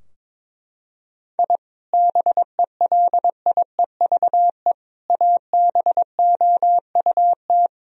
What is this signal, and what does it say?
Morse code, heard as a keyed tone, about 22 wpm, 700 Hz. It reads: I BELIEVE ABOUT
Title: Sentences from Top 1000 Words - Review Part 3 - 22wpm